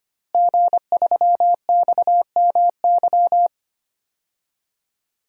Morse code, heard as a keyed tone, about 25 wpm, 700 Hz. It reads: G3XMY
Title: DXCC-100 Call Signs - 5 Characters - Repeated 2x - 25wpm